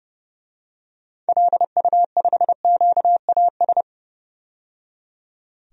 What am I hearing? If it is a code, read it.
Morse code, heard as a keyed tone, about 30 wpm, 700 Hz. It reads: LU5QAS
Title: DXCC-100 Call Signs - 6 Characters - 30wpm